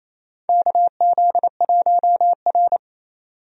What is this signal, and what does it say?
Morse code, heard as a keyed tone, about 28 wpm, 700 Hz. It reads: KZ1R
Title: USA Call Signs - 4 Characters - 28wpm